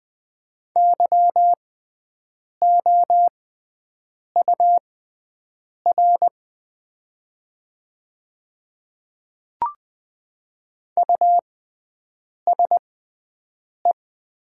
Morse code, heard as a keyed tone, about 20 wpm, spaced as for 7 wpm, 690 Hz. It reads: YOUR USE
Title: 6x Character Spacing - Top 100 Words - 20wpm